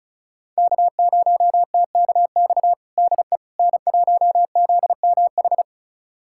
Morse code, heard as a keyed tone, about 34 wpm, 700 Hz. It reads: K0TKX DE N1ZMH
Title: USA Callsign DE Callsign - 5 Characters - 35wpm